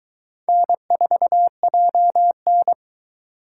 Morse code, heard as a keyed tone, about 22 wpm, 700 Hz. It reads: N4JN